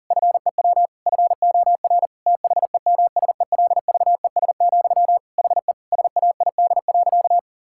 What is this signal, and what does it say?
Morse code, heard as a keyed tone, about 37 wpm, 700 Hz. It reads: FEW FOR THEMSELVES, HE SAID.
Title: Sentences from Top 1000 Words - Review Part 6 - 40wpm